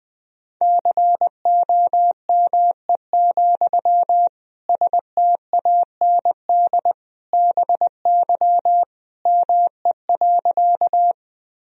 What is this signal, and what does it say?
Morse code, heard as a keyed tone, about 20 wpm, 700 Hz. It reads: COME, STAND BY ME.
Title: Sentences from Top 1000 Words - Review Part 1 - 20wpm